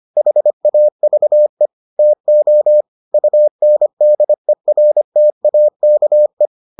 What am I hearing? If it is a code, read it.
Morse code, heard as a keyed tone, about 25 wpm, 600 Hz. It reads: HAVE TO UNDERTAKE